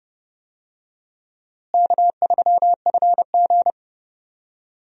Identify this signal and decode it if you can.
Morse code, heard as a keyed tone, about 30 wpm, 700 Hz. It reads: K3FG
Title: USA Call Signs - 4 Characters - Repeated 2x - 30wpm